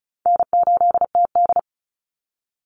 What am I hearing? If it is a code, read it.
Morse code, heard as a keyed tone, about 34 wpm, 700 Hz. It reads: N8TD
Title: Call Signs - 35wpm